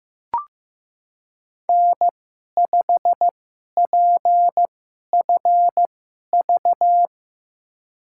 Morse code, heard as a keyed tone, about 15 wpm, spaced as for 11 wpm, 700 Hz. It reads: N5PFV
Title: USA Call Signs - 2x Character Spacing - 15wpm